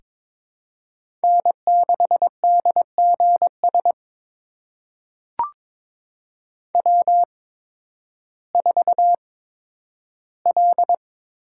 Morse code, heard as a keyed tone, about 22 wpm, spaced as for 12 wpm, 700 Hz. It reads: N6DGS W4L
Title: USA Call Signs - 5 Characters - 8x Character Spacing - 22wpm